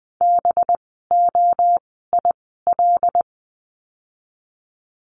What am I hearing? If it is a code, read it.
Morse code, heard as a keyed tone, about 20 wpm, spaced as for 14 wpm, 700 Hz. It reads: BOIL
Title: 185 Lesson 30f - Letter b - Words 20wpm